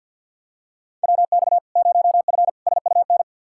Morse code, heard as a keyed tone, about 52 wpm, 700 Hz. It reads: WX 0F SUN
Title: Practice QSO - Exchange 2 - Info - 50wpm